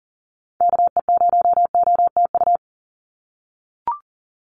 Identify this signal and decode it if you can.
Morse code, heard as a keyed tone, about 40 wpm, 690 Hz. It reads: KE0OTU
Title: USA Call Signs 40wpm